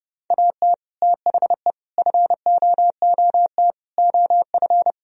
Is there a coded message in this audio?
Yes: Morse code from 0.3 to 4.9 s, heard as a keyed tone, about 30 wpm, 700 Hz. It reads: AT THE FOOT OF